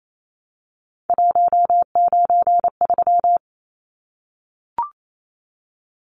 Morse code, heard as a keyed tone, about 28 wpm, 700 Hz. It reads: 193
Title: Numbers - 3-Digit 28wpm